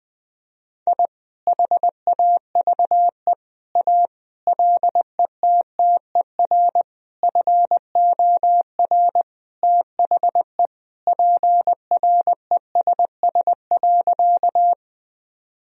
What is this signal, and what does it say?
Morse code, heard as a keyed tone, about 20 wpm, 700 Hz. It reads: I HAVE A LETTER FOR THE PRESS.
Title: Sentences from Top 1000 Words - Review Part 1 - 20wpm